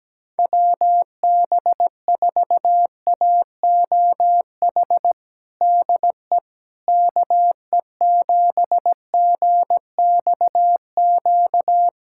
Morse code, heard as a keyed tone, about 17 wpm, 700 Hz. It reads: WB4AOH DE KE7GXQ
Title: USA Callsign DE Callsign 17wpm